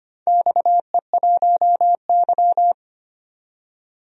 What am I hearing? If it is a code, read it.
Morse code, heard as a keyed tone, about 25 wpm, 700 Hz. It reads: XE1Y
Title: DXCC-100 Call Signs - 4 Characters - 25wpm